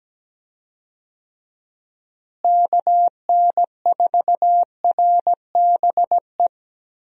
Morse code, heard as a keyed tone, about 17 wpm, 700 Hz. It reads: KN4RBE
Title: USA Call Signs - 6 Characters - Repeated 3x - 17wpm